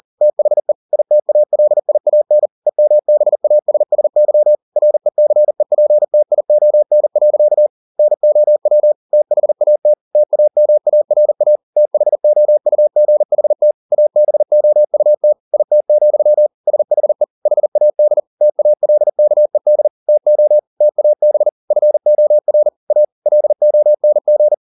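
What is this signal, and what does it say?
Morse code, heard as a keyed tone, about 40 wpm, 600 Hz. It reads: THE ITALIAN EMBASSY RECEPTION. NOW THAT TAMARA THOUGHT ABOUT IT, SHE HAD TALKED TO TAB FOR A LONG
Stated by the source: Morse code at 40 wpm, practice text, CW med Gnister.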